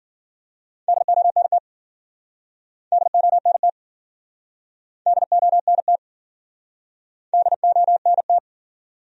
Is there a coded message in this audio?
Yes: Morse code from 0.9 to 8.4 s, heard as a keyed tone, about 55 wpm, 700 Hz. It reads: DONT DONT DONT DONT